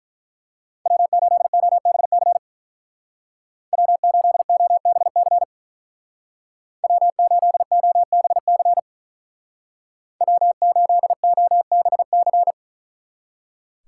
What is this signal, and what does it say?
Morse code, heard as a keyed tone, about 44 wpm, 700 Hz. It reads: W8OBC W8OBC W8OBC W8OBC